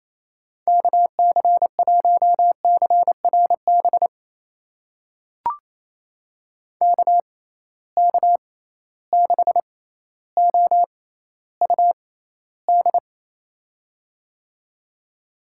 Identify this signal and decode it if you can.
Morse code, heard as a keyed tone, about 27 wpm, spaced as for 14 wpm, 700 Hz. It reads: KC1CRB KK6OUD
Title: USA Call Signs - 6 Characters - 6x Character Spacing - 28wpm